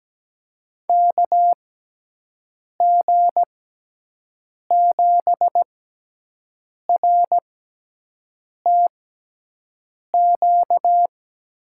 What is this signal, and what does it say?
Morse code, heard as a keyed tone, about 17 wpm, spaced as for 6 wpm, 700 Hz. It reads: KG7RTQ
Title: USA Call Signs - 6 Characters - 6x Character Spacing - 17wpm